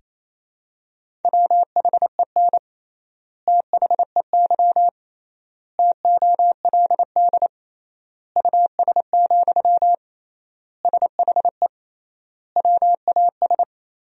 Morse code, heard as a keyed tone, about 28 wpm, 700 Hz. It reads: WHEN THEY TOLD US, SHE WAS
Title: 3x Word-Spacing - Sentences from Top 400 Words - 28wpm